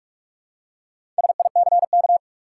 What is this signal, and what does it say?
Morse code, heard as a keyed tone, about 47 wpm, 700 Hz. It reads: SICK
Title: Four Letter Words - 45wpm